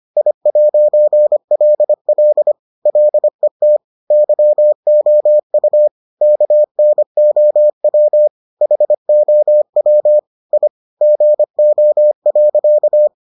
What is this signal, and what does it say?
Morse code, heard as a keyed tone, about 25 wpm, 600 Hz. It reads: I'LL LET YOU KNOW HOW I GO.